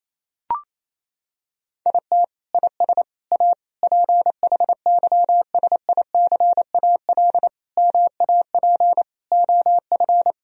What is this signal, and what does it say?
Morse code, heard as a keyed tone, about 28 wpm, 700 Hz. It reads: IT IS A PHYSICAL MAP OF
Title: Sentences from Top 1000 Words - Review Part 4 - 28wpm